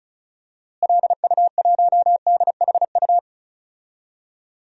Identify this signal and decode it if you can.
Morse code, heard as a keyed tone, about 34 wpm, 700 Hz. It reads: LU1DHU
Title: DXCC-100 Call Signs - 6 Characters - Repeated 2x - 35wpm